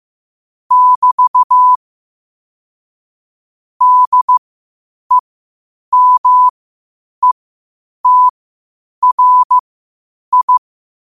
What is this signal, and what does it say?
Morse code, heard as a keyed tone, about 15 wpm, spaced as for 8 wpm, 1000 Hz. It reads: <BT> DEMETRI
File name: qotdfiles / 8wpm